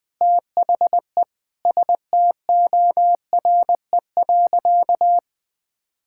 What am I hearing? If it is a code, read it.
Morse code, heard as a keyed tone, about 20 wpm, 700 Hz. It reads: THE STORE.
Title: Sentences with 8-Word Phrases - 20wpm